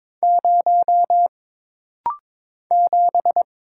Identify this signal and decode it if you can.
Morse code, heard as a keyed tone, about 22 wpm, 700 Hz. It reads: 0 7
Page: This practice set is composed of Letters, Numbers, common punctuations, and common prosigns.